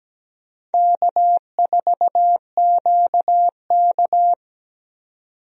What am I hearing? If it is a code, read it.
Morse code, heard as a keyed tone, about 17 wpm, 700 Hz. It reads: K4QK
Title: USA Call Signs - 4 Characters - 17wpm